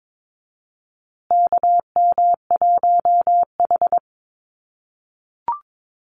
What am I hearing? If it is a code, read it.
Morse code, heard as a keyed tone, about 22 wpm, 700 Hz. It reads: KM1H
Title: USA Call Signs Repeated 3x 22wpm